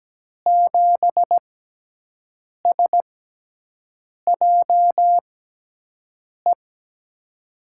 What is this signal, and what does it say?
Morse code, heard as a keyed tone, about 17 wpm, spaced as for 6 wpm, 700 Hz. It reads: 7SJE
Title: USA Call Signs - 6x Character Spacing - 17wpm